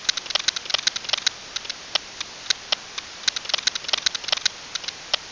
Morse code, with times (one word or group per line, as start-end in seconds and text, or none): none